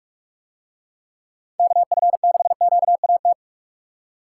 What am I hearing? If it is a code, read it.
Morse code, heard as a keyed tone, about 42 wpm, 700 Hz. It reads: KF6QAT